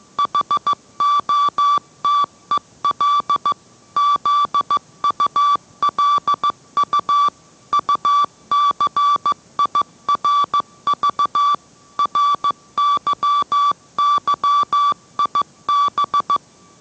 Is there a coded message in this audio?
Yes: Morse code from 0.2 to 16.4 s, heard as a keyed tone, about 15 wpm, 1160 Hz. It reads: HOTEL ZULU UCIRV RYYIB